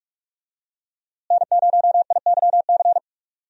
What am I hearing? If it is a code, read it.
Morse code, heard as a keyed tone, about 42 wpm, 700 Hz. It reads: N0IYC
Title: USA Call Signs - 5 Characters - Repeated 3x - 45wpm